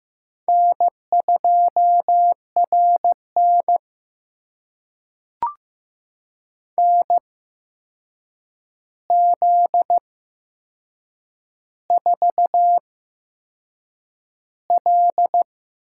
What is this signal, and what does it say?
Morse code, heard as a keyed tone, about 15 wpm, spaced as for 6 wpm, 700 Hz. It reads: N2RN NZ4L
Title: USA Call Signs - 8x Character Spacing - 15wpm